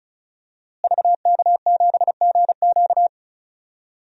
Morse code, heard as a keyed tone, about 36 wpm, 700 Hz. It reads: VK7GQ
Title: DXCC-100 Call Signs - 5 Characters - Repeated 2x - 35wpm